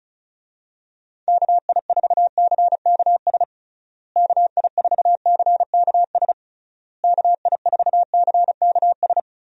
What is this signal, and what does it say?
Morse code, heard as a keyed tone, about 36 wpm, 700 Hz. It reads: KI4CKS KI4CKS KI4CKS
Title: USA Call Signs - 6 Characters - Repeated 3x - 35wpm